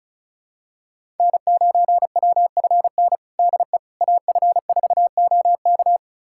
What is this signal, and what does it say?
Morse code, heard as a keyed tone, about 36 wpm, 700 Hz. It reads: N9WFN DE AF4OK
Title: USA Callsign DE Callsign - 5 Characters - 35wpm